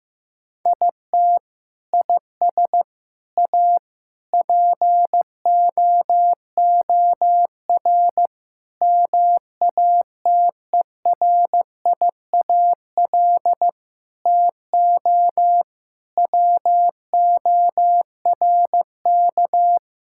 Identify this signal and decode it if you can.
Morse code, heard as a keyed tone, about 15 wpm, 700 Hz. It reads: IT IS A POOR MATERIAL TO WORK
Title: Sentences from Top 1000 Words - Review Part 5 - 15wpm